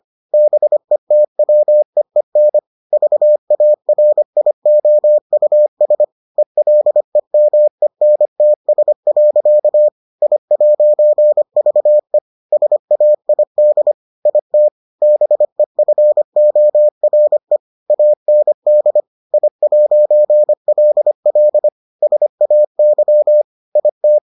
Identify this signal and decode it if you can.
Morse code, heard as a keyed tone, about 25 wpm, 600 Hz. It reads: BETWEEN VARIOUS ELEMENTS. I'VE SAID IT BEFORE AND I'LL SAY IT